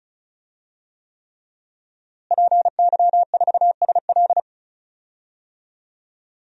Morse code, heard as a keyed tone, about 36 wpm, 700 Hz. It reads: PY4SL
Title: DXCC-100 Call Signs - 5 Characters - Repeated 3x - 35wpm